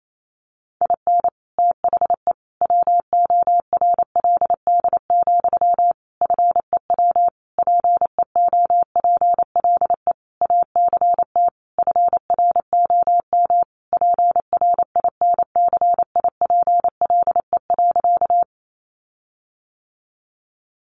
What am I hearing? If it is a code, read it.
Morse code, heard as a keyed tone, about 28 wpm, 700 Hz. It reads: IN THE WORLD, FEW PEOPLE ACT FROM PRINCIPLE.
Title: Sentences from Top 1400 Words 28wpm